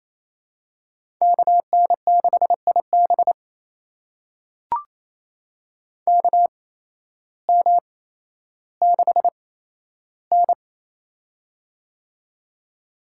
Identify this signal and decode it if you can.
Morse code, heard as a keyed tone, about 28 wpm, spaced as for 13 wpm, 700 Hz. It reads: KN6IB KM6N
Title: USA Call Signs - 8x Character Spacing - 28wpm